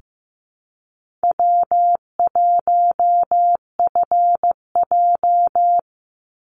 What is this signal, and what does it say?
Morse code, heard as a keyed tone, about 15 wpm, 700 Hz. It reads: W1FJ
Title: Easy Call Signs - 15wpm